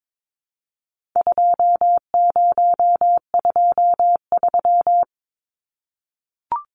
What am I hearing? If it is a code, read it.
Morse code, heard as a keyed tone, about 22 wpm, 700 Hz. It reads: 2023
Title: Numbers - 4-Digit 22wpm